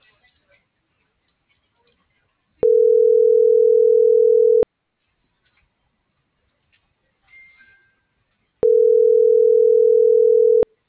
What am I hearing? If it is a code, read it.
Ringback tone (North America), 440 + 480 Hz.